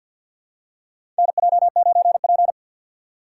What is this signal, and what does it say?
Morse code, heard as a keyed tone, about 50 wpm, 700 Hz. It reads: NJ9P
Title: USA Call Signs - 4 Characters - 3x Character Spacing - 50wpm